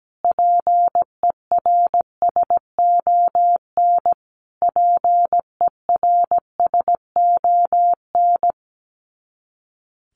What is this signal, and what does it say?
Morse code, heard as a keyed tone, about 17 wpm, 700 Hz. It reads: PERSON PERSON
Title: Top 500 Words Repeated 17wpm